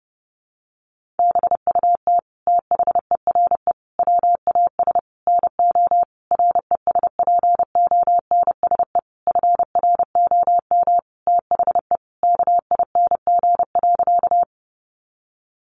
Morse code, heard as a keyed tone, about 30 wpm, 700 Hz. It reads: BUT THERE WAS NO RESPONSE FROM THE KING.
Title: Sentences from Top 1400 Words 30wpm